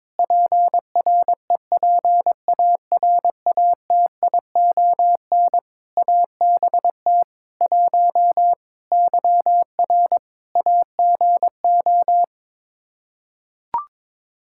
Morse code, heard as a keyed tone, about 22 wpm, 700 Hz. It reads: PREPARATION ABT 1 YR AGO
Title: Practice QSO - Exchange 3 - More Info - 22wpm